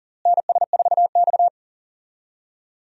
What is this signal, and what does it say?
Morse code, heard as a keyed tone, about 40 wpm, 700 Hz. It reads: NS4X
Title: USA Call Signs - 4 Characters - Repeated 3x - 40wpm